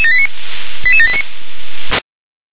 Five-tone selective call (ZVEI), heard as quick runs of five tones, tones of about 70 ms. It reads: 88789
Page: /357/SDR/ZVEI/88652/